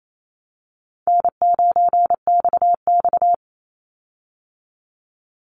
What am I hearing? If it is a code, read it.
Morse code, heard as a keyed tone, about 28 wpm, 700 Hz. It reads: N9XX
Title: Easy Call Signs Repeated 3x - 28wpm